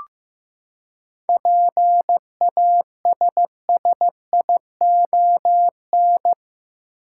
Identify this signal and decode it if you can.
Morse code, heard as a keyed tone, about 16 wpm, 700 Hz. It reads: PASSION
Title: Seven Letter Words - 15wpm